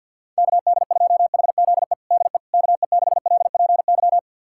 Morse code, heard as a keyed tone, about 52 wpm, 700 Hz. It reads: KD2HZE DE KE6LPY
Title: USA Callsign DE Callsign - 6 Characters - 50wpm